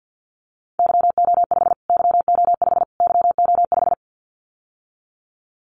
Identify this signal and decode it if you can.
Morse code, heard as a keyed tone, about 50 wpm, 700 Hz. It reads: YO5 YO5 YO5
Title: Callsign Prefixes for Top 25 DXCC - Repeated 3x - 50wpm